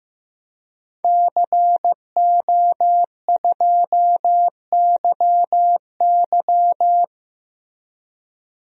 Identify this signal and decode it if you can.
Morse code, heard as a keyed tone, about 15 wpm, 700 Hz. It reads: CO2YY
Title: DXCC-100 Call Signs - 5 Characters - Repeated 3x - 15wpm